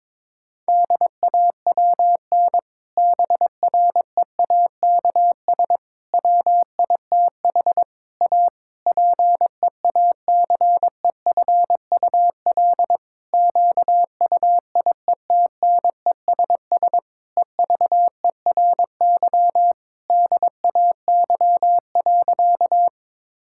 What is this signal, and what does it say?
Morse code, heard as a keyed tone, about 22 wpm, 700 Hz. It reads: DAWN BREAKS WITH A PEACEFUL QUIETNESS EVERY DAY.
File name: Difficult Sentences with Hints - 22wpm